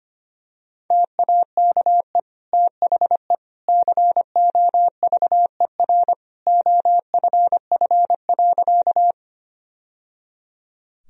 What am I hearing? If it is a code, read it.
Morse code, heard as a keyed tone, about 25 wpm, 700 Hz. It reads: TAKE THE COVER OFF.
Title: Sentences from Top 1000 Words - Review Part 1 - 25wpm